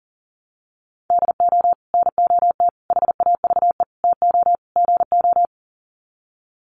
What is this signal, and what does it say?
Morse code, heard as a keyed tone, about 40 wpm, 700 Hz. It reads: DO NOT HAVE TO GO
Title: Sets of 5 Words 40wpm